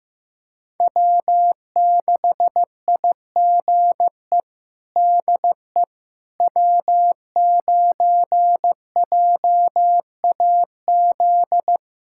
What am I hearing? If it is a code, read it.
Morse code, heard as a keyed tone, about 15 wpm, 700 Hz. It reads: W6IGE DE W9JAZ
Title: USA Callsign DE Callsign - 5 Characters - 15wpm